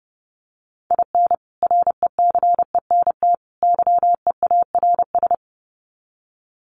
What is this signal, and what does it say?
Morse code, heard as a keyed tone, about 30 wpm, 700 Hz. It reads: IN RECENT YEARS
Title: Sets of 3 Words 30wpm